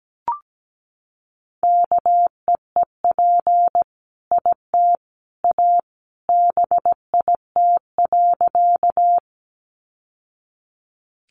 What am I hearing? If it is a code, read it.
Morse code, heard as a keyed tone, about 17 wpm, 700 Hz. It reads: KEEP IT A BIT.
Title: Sentences from Top 900 Words 17wpm